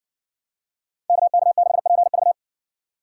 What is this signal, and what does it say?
Morse code, heard as a keyed tone, about 60 wpm, 700 Hz. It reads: KK6PV